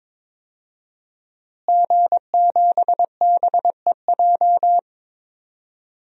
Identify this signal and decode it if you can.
Morse code, heard as a keyed tone, about 22 wpm, 700 Hz. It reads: G7BEJ
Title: DXCC-100 Call Signs - 5 Characters - Repeated 2x - 22wpm